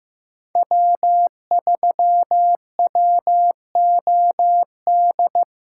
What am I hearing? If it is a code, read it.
Morse code, heard as a keyed tone, about 15 wpm, 700 Hz. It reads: W3WOD
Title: USA Call Signs - 5 Characters - 2x Character Spacing - 15wpm